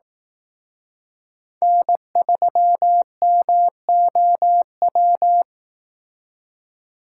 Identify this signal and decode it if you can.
Morse code, heard as a keyed tone, about 18 wpm, 700 Hz. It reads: N3MOW
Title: Speed Racing - USA 1x3 Call Signs 15wpm